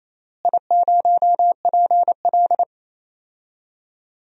Morse code, heard as a keyed tone, about 27 wpm, 700 Hz. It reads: I0PL
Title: DXCC-100 Call Signs - 4 Characters - Repeated 3x - 28wpm